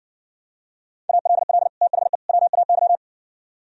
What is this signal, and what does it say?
Morse code, heard as a keyed tone, about 59 wpm, 700 Hz. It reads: ALL THE WAY